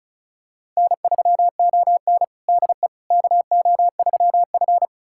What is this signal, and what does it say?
Morse code, heard as a keyed tone, about 34 wpm, 700 Hz. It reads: N3ON DE KO3F
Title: USA Callsign DE Callsign - 4 Characters - 35wpm